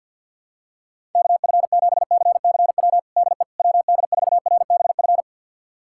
Morse code, heard as a keyed tone, about 52 wpm, 700 Hz. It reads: KF7CCW DE WD4RBF
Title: USA Callsign DE Callsign - 6 Characters - 50wpm